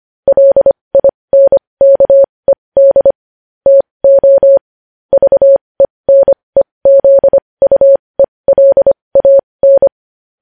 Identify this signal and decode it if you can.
Morse code, heard as a keyed tone, about 25 wpm, spaced as for 20 wpm, 550 Hz. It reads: LINKED TO VENEZUELAN
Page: news headlines in morse code at 20 wpm